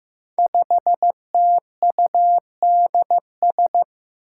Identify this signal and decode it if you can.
Morse code, heard as a keyed tone, about 15 wpm, 700 Hz. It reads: 5TUDS
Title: Five Challenging Letter-Number Combinations 15wpm